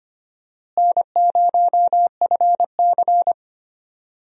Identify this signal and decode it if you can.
Morse code, heard as a keyed tone, about 25 wpm, 700 Hz. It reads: N0FC
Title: USA Call Signs - 4 Characters - Repeated 2x - 25wpm